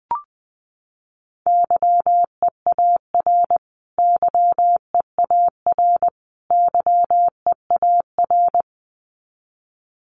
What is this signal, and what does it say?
Morse code, heard as a keyed tone, about 20 wpm, 700 Hz. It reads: YEAR YEAR YEAR
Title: Top 200 Words Repeated 20wpm